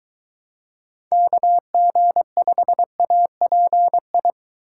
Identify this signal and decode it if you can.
Morse code, heard as a keyed tone, about 22 wpm, 700 Hz. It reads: KG5API